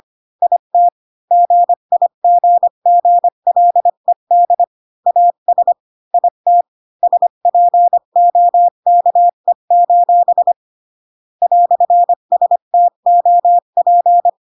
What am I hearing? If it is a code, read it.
Morse code, heard as a keyed tone, about 25 wpm, 700 Hz. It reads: IT GIGGLED AS IT SPOKE: "STOP
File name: Pinocchio / Pinocchio ch1 20 25 ik1hge